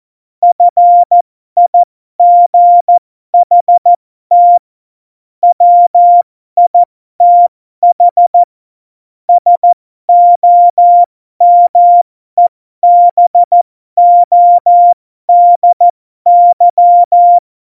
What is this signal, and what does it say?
Morse code, heard as a keyed tone, about 14 wpm, spaced as for 12 wpm, 700 Hz. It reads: FIGHT WITH SOMEBODY
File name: Morsekurs / Witze